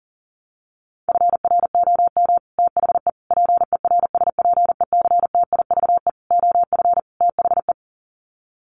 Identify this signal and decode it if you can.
Morse code, heard as a keyed tone, about 40 wpm, 700 Hz. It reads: FROM THE PERSPECTIVE OF THE
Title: Sets of 5 Words Encore2 40wpm